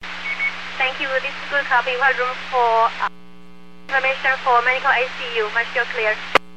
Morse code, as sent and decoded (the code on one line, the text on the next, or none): none